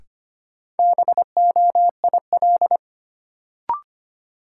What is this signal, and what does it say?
Morse code, heard as a keyed tone, about 25 wpm, 700 Hz. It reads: BOIL
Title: Sets of 5 Words Encore - New Word Drill - 25wpm